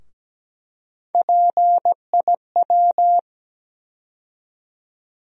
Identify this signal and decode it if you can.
Morse code, heard as a keyed tone, about 17 wpm, 700 Hz. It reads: PIW
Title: USA Call Sign Postfixes - 3 Letters 17wpm